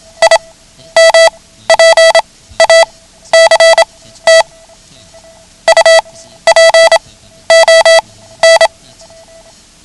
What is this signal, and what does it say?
Morse code, heard as a keyed tone, about 27 wpm, spaced as for 13 wpm, 660 Hz. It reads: IMPACT UPON
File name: ham radio / CodeQuick / Code Quick 09